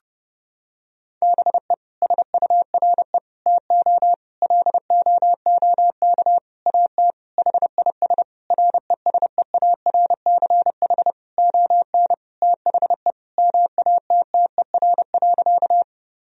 Morse code, heard as a keyed tone, about 30 wpm, 700 Hz. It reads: BE SURE TO LOOK AT HIS RESEARCH ON THE MATTER.
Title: Sentences from Top 1000 Words - Review Part 4 - 30wpm